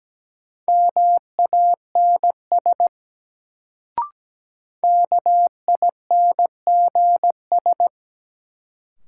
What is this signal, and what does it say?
Morse code, heard as a keyed tone, about 17 wpm, 700 Hz. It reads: MANS KINGS
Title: Sentences from Top 1000 Words - Review Part 6 - New Word Drill - 17wpm